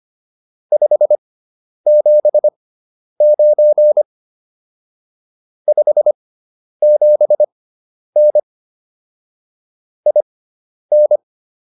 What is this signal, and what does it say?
Morse code, heard as a keyed tone, about 25 wpm, spaced as for 10 wpm, 600 Hz. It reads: 579 57N IN
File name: QSO301 10